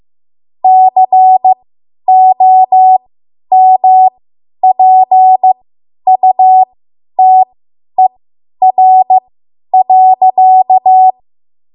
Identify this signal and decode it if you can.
Morse code, heard as a keyed tone, about 15 wpm, spaced as for 10 wpm, 750 Hz. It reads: COMPUTER.
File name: Morse / 10WPM